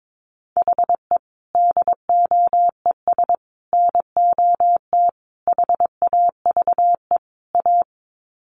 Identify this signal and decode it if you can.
Morse code, heard as a keyed tone, about 22 wpm, 700 Hz. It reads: HE DOES NOT HAVE A
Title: Sets of 5 Words Encore 22wpm